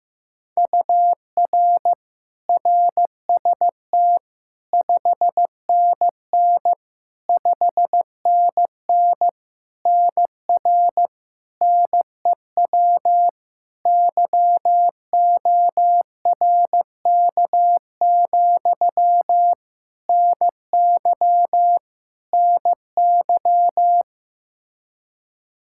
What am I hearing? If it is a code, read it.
Morse code, heard as a keyed tone, about 15 wpm, 700 Hz. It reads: UR RST 5NN 5NN NR NEW YORK, NY NY
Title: K1USN SST QSO Practice 15wpm